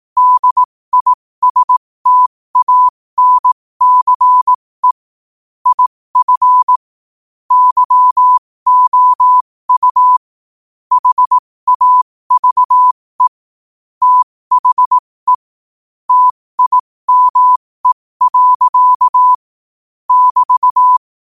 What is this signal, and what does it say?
Morse code, heard as a keyed tone, about 18 wpm, spaced as for 15 wpm, 1000 Hz. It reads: DISTANCE IF YOU HAVE THE TIME. <BT>
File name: qotdfiles / 15wpm